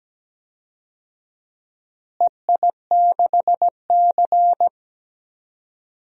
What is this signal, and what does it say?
Morse code, heard as a keyed tone, about 17 wpm, 700 Hz. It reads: EI6C
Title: DXCC-100 Call Signs - 4 Characters - Repeated 2x - 17wpm